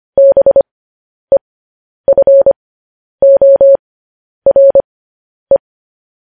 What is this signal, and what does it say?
Morse code, heard as a keyed tone, about 25 wpm, spaced as for 10 wpm, 550 Hz. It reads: BEFORE
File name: podcast / News headlines-1x-25x10